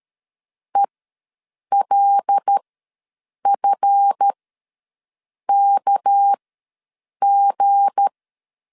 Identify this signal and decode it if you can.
Morse code, heard as a keyed tone, about 13 wpm, spaced as for 7 wpm, 800 Hz. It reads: ELFKG
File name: LEZIONI / 7 GF9Z / 7 GF9Z ws 12wpm 3321